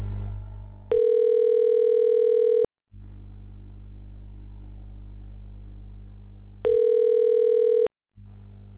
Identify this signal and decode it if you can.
Ringback tone (North America), 440 + 480 Hz.